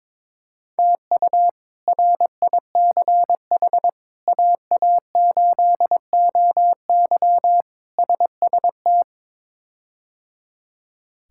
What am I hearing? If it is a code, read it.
Morse code, heard as a keyed tone, about 21 wpm, 700 Hz. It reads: TU RICH AA8OY SST
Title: K1USN SST QSO Practice - 22wpm